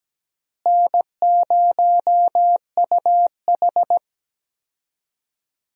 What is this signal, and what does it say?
Morse code, heard as a keyed tone, about 17 wpm, 700 Hz. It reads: N0UH
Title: USA Call Signs - 4 Characters - Repeated 2x - 17wpm